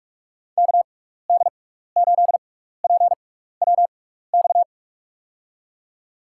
Morse code, heard as a keyed tone, about 44 wpm, spaced as for 15 wpm, 700 Hz. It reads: KD8PWX
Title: USA Call Signs - 6 Characters - 6x Character Spacing - 45wpm